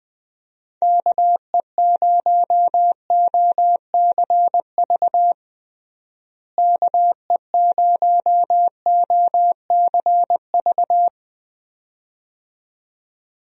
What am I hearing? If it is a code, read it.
Morse code, heard as a keyed tone, about 20 wpm, 700 Hz. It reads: KE0OCV KE0OCV
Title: USA Call Signs - 6 Characters - Repeated 2x - 20wpm